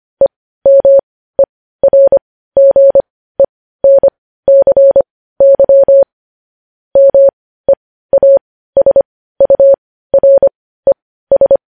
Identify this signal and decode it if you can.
Morse code, heard as a keyed tone, about 25 wpm, spaced as for 15 wpm, 550 Hz. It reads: EMERGENCY MEASURES